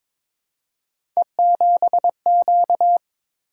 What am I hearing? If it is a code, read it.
Morse code, heard as a keyed tone, about 22 wpm, 700 Hz. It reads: E7Q
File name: practice / 22wpm / Three Letter-Number Combinations - 22wpm